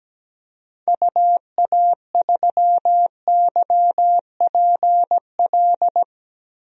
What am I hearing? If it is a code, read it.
Morse code, heard as a keyed tone, about 17 wpm, 700 Hz. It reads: UA3YPL
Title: DXCC-100 Call Signs - 6 Characters - 17wpm